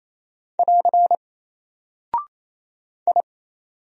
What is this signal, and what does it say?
Morse code, heard as a keyed tone, about 28 wpm, 700 Hz. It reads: <AR> I
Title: Single Character (Letter:Number:Punctuation:Prosign) Alt 28wpm